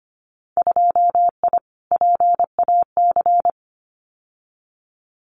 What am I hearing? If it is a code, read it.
Morse code, heard as a keyed tone, about 25 wpm, 700 Hz. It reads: 2I PAC
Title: Winter Field Day Exchanges Encore 25wpm